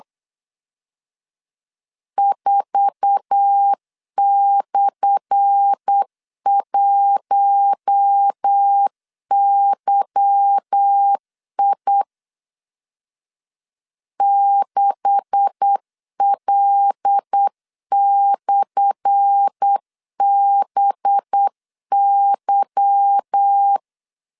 Morse code, heard as a keyed tone, about 8 wpm, 800 Hz. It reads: 4/1YI 6L/BY
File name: LEZIONI / 10 SIGNS / 10A SLASH 8wpm 3452